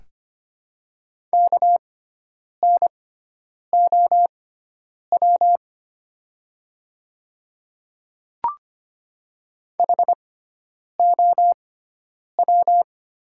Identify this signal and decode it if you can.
Morse code, heard as a keyed tone, about 25 wpm, spaced as for 9 wpm, 700 Hz. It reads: KNOW HOW
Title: 6x Character Spacing - Top 100 Words - 25wpm